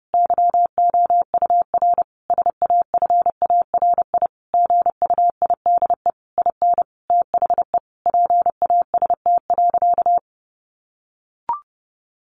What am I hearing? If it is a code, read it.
Morse code, heard as a keyed tone, about 30 wpm, 700 Hz. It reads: YOUR SAFARI GUIDE IN THE PAST.
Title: A Sound Of Thunder 30wpm